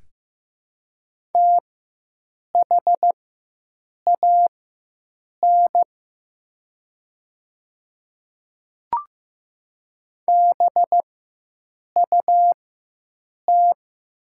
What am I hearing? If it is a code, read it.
Morse code, heard as a keyed tone, about 15 wpm, spaced as for 7 wpm, 700 Hz. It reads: THAN BUT
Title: 4x Character Spacing - Top 100 Words - 15wpm